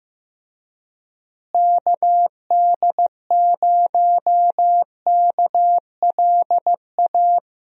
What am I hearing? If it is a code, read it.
Morse code, heard as a keyed tone, about 15 wpm, 700 Hz. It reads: KD0KLA
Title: USA Call Signs - 6 Characters - Repeated 3x - 15wpm